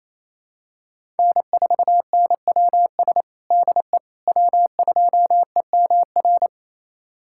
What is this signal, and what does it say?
Morse code, heard as a keyed tone, about 28 wpm, 700 Hz. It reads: N4NWS DE W2EMR
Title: USA Callsign DE Callsign - 5 Characters - 28wpm